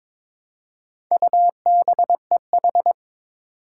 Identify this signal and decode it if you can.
Morse code, heard as a keyed tone, about 22 wpm, 700 Hz. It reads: UBEH
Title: Four Challenging Letter-Number Combinations 22wpm